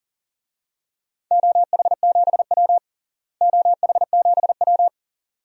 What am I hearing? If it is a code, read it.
Morse code, heard as a keyed tone, about 40 wpm, 700 Hz. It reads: OH7W OH7W
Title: DXCC-100 Call Signs - 4 Characters - Repeated 2x - 40wpm